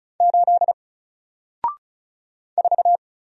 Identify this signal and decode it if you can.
Morse code, heard as a keyed tone, about 35 wpm, 700 Hz. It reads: 8 4
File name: Single-Letter-Number-35wpm